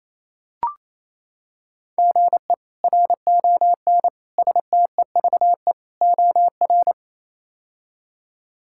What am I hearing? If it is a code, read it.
Morse code, heard as a keyed tone, about 28 wpm, 700 Hz. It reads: GE RON STEVE OR
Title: K1USN SST QSO Practice - 28wpm